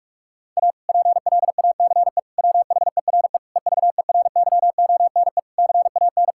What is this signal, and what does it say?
Morse code, heard as a keyed tone, about 45 wpm, 700 Hz. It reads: A PLACE WHERE EVERYONE CAN
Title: Sentences from Top 1000 Words - Review Part 5 - 45wpm